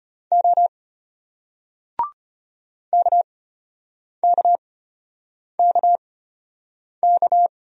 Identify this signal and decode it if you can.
Morse code, heard as a keyed tone, about 31 wpm, spaced as for 8 wpm, 700 Hz. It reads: O KKKK
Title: Speed-Racing Single Letters 25wpm